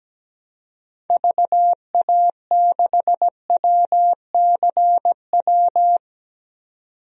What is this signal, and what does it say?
Morse code, heard as a keyed tone, about 17 wpm, 700 Hz. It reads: VA6WCW
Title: DXCC-100 Call Signs - 6 Characters - 17wpm